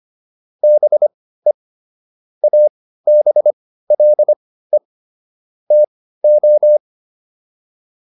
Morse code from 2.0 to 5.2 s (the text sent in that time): ABLE